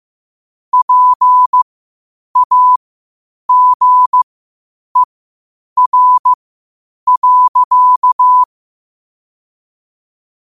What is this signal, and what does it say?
Morse code, heard as a keyed tone, about 15 wpm, spaced as for 8 wpm, 1000 Hz. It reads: PAGER.
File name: qotdfiles / 8wpm